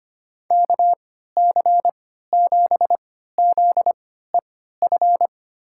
Morse code, heard as a keyed tone, about 25 wpm, spaced as for 14 wpm, 700 Hz. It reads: KC7ZEF
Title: USA Call Signs - 6 Characters - 3x Character Spacing - 25wpm